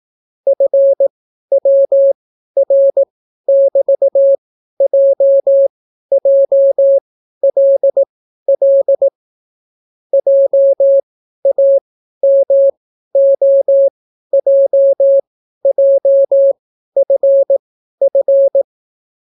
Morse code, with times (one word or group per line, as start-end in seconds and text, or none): 0.5-9.1 FWR<BT>JJLL
10.1-18.6 JAMOJJFF